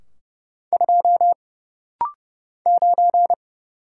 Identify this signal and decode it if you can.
Morse code, heard as a keyed tone, about 30 wpm, 700 Hz. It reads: 2 9
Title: Single Letter-Number Alt - Rapid-Fire 30wpm